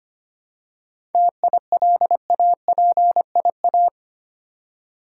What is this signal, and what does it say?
Morse code, heard as a keyed tone, about 25 wpm, 700 Hz. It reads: TILAPIA
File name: practice / 25wpm / Difficult Words with Hints - 25wpm